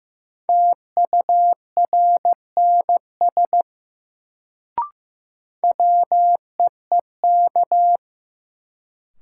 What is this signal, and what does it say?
Morse code, heard as a keyed tone, about 15 wpm, 700 Hz. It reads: TURNS WEEK
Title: Sets of 5 Words Encore - New Word Drill - 15wpm